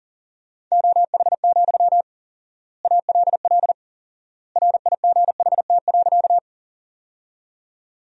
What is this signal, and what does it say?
Morse code, heard as a keyed tone, about 40 wpm, 700 Hz. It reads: OH, ALL RIGHT.
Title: 4x Word-Spacing - Sentences from Top 500 Words - 40wpm